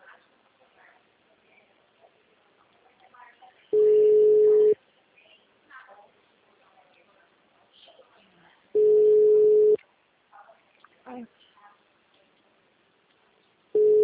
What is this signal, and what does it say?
Ringback tone (Europe, 425 Hz), with speech.